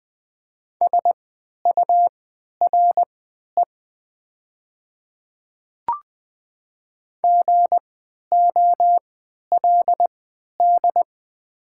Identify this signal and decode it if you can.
Morse code, heard as a keyed tone, about 20 wpm, spaced as for 11 wpm, 700 Hz. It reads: SURE GOLD
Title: 3x Character Spacing - Top 500 Words - 20wpm